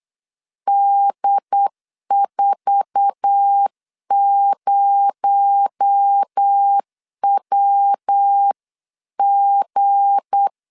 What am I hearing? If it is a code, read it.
Morse code, heard as a keyed tone, about 8 wpm, 800 Hz. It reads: D40WG
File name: CW / DET / 1 9 1500C S B 8wpm 5624